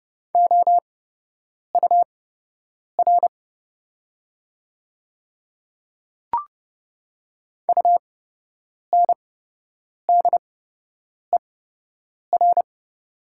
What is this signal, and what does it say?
Morse code, heard as a keyed tone, about 30 wpm, spaced as for 8 wpm, 700 Hz. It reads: OUR UNDER
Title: 8x Character Spacing - Top 200 Words - 30wpm